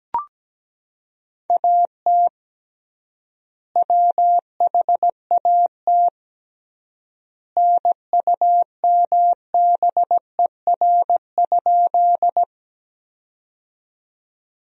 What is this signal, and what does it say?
Morse code, heard as a keyed tone, about 17 wpm, 700 Hz. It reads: AT WHAT NUMBER?
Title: 3x Word-Spacing - Sentences from Top 100 Words - 17wpm